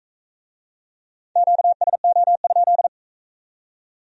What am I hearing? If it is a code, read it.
Morse code, heard as a keyed tone, about 41 wpm, 700 Hz. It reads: QSO?